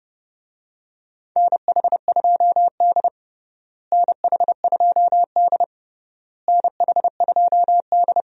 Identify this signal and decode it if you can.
Morse code, heard as a keyed tone, about 32 wpm, 700 Hz. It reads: NH2D NH2D NH2D
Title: USA Call Signs - 4 Characters - Repeated 3x - 30wpm